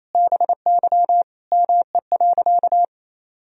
Morse code, heard as a keyed tone, about 28 wpm, 700 Hz. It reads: BY ME.
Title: Sentences from Top 1000 Words - Review Part 1 - 28wpm